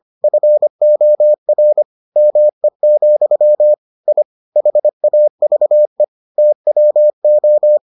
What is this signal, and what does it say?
Morse code, heard as a keyed tone, about 25 wpm, 600 Hz. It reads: FOR ME, I HAVE TWO